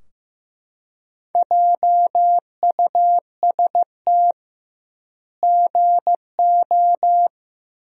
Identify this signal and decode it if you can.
Morse code, heard as a keyed tone, about 15 wpm, 700 Hz. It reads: JUST GO
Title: 2x Word-Spacing - Sentences from Top 500 Words - 15wpm